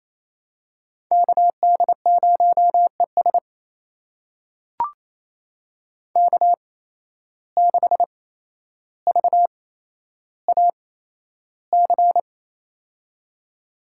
Morse code, heard as a keyed tone, about 28 wpm, spaced as for 12 wpm, 700 Hz. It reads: KD0ES K6VAC
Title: USA Call Signs - 5 Characters - 8x Character Spacing - 28wpm